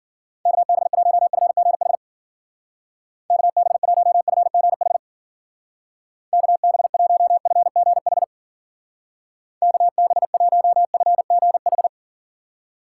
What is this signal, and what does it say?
Morse code, heard as a keyed tone, about 53 wpm, 700 Hz. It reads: KB1FGH KB1FGH KB1FGH KB1FGH